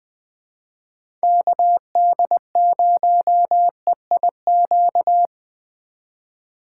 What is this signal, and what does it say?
Morse code, heard as a keyed tone, about 20 wpm, 700 Hz. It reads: KD0EIQ
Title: Speed Racing - USA 2x3 Call Signs 17wpm